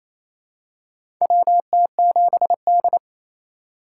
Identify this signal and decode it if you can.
Morse code, heard as a keyed tone, about 28 wpm, 700 Hz. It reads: WT7D
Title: USA Call Signs - 4 Characters - 3x Character Spacing - 28wpm